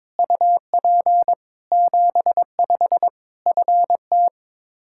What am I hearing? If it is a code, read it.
Morse code, heard as a keyed tone, about 22 wpm, 700 Hz. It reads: UP 75 FT
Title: K1USN SST QSO Practice 22wpm